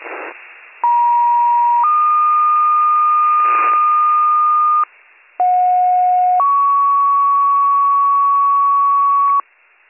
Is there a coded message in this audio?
Yes: two-tone page from 0.8 to 9.4 s, heard as a one-second tone, then a three-second tone, about 950 then 1220 Hz, about 730 then 1120 Hz.